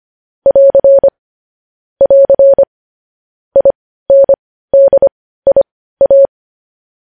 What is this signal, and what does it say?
Morse code, heard as a keyed tone, about 25 wpm, spaced as for 15 wpm, 550 Hz. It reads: <AR> <AR> INDIA
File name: podcast / News headlines-1x-25x15